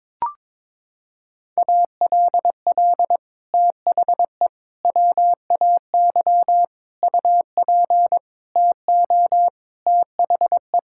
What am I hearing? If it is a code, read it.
Morse code, heard as a keyed tone, about 22 wpm, 700 Hz. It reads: ALL THE WAY UP TO THE
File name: practice / 22wpm / Sets of 6 Words - 22wpm